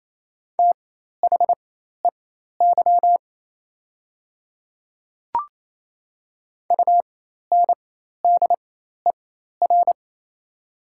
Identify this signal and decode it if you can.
Morse code, heard as a keyed tone, about 28 wpm, spaced as for 13 wpm, 690 Hz. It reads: THEY UNDER
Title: 4x Character Spacing - Top 300 Words - 28wpm